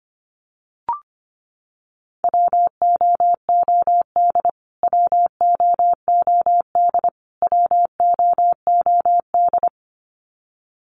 Morse code, heard as a keyed tone, about 25 wpm, 700 Hz. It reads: WOOD WOOD WOOD
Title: Top 500 Words Repeated 25wpm